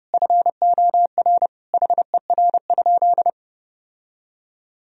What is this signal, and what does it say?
Morse code, heard as a keyed tone, about 30 wpm, 700 Hz. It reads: FOR HER?
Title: Sentences from Top 1000 Words - Review Part 1 - 30wpm